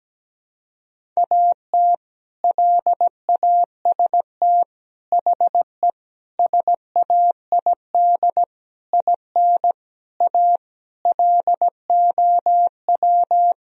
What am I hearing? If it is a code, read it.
Morse code, heard as a keyed tone, about 17 wpm, 700 Hz. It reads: AT LAST HE SAID IN A LOW
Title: Sentences from Top 1000 Words - Review Part 1 - 17wpm